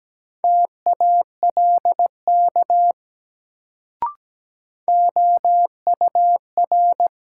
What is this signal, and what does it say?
Morse code, heard as a keyed tone, about 17 wpm, 700 Hz. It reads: TALK OUR
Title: Sets of 2 Words - New Word Drill - 17wpm